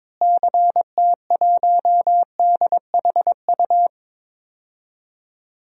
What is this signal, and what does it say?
Morse code, heard as a keyed tone, about 21 wpm, 700 Hz. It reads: CT1DHU
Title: DXCC-100 Call Signs - 6 Characters - Repeated 3x - 22wpm